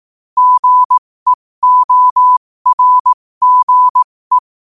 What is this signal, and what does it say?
Morse code, heard as a keyed tone, about 18 wpm, spaced as for 15 wpm, 1000 Hz. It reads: GEORGE
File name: qotdfiles / 15wpm